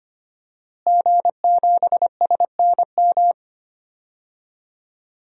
Morse code, heard as a keyed tone, about 25 wpm, 700 Hz. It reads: G7SNM
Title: DXCC-100 Call Signs - 5 Characters - Repeated 2x - 25wpm